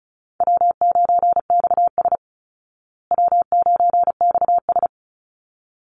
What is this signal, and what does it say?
Morse code, heard as a keyed tone, about 34 wpm, 700 Hz. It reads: W9XS W9XS
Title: Easy Call Signs Repeated 2x - 35wpm